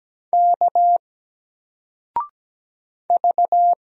Morse code, heard as a keyed tone, about 17 wpm, 700 Hz. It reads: K V
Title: Single Letter-Number 17wpm